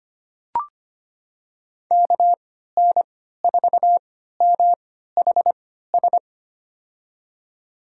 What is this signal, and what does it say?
Morse code, heard as a keyed tone, about 25 wpm, spaced as for 14 wpm, 700 Hz. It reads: KN4MHS
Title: USA Call Signs - 6 Characters - 3x Character Spacing - 25wpm